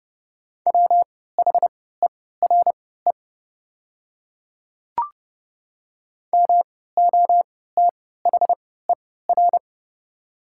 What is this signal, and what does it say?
Morse code, heard as a keyed tone, about 29 wpm, spaced as for 17 wpm, 700 Hz. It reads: WHERE MOTHER
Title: 3x Character Spacing - Top 200 Words - 30wpm